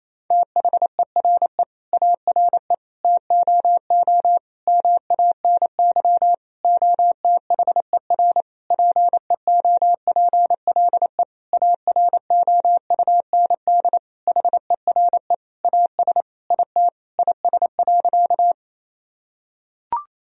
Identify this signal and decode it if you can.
Morse code, heard as a keyed tone, about 28 wpm, 700 Hz. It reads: THERE ARE TOO MANY OTHER PEOPLE AROUND HERE AS IT IS.
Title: Sentences from Top 1000 Words - Review Part 3 - 28wpm